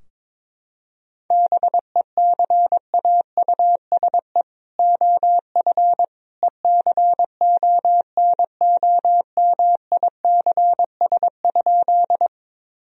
Morse code, heard as a keyed tone, about 22 wpm, 700 Hz. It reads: BECAUSE OF ECONOMICS?
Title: Sentences from Top 1000 Words - Review Part 3 - 22wpm